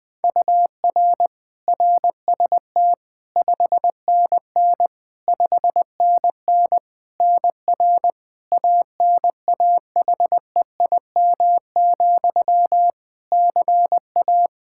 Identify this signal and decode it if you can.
Morse code, heard as a keyed tone, about 21 wpm, 700 Hz. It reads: UR RST 5NN 5NN NR ANAHEIM, CA
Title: Practice QSO - Exchange 1 - Beginning - 20wpm